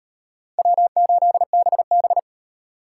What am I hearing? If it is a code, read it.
Morse code, heard as a keyed tone, about 36 wpm, 700 Hz. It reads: W8BB